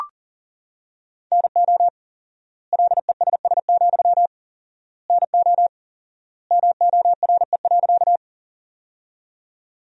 Morse code, heard as a keyed tone, about 40 wpm, 700 Hz. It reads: NO LESS, NO MORE.
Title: 4x Word-Spacing - Sentences from Top 400 Words - 40wpm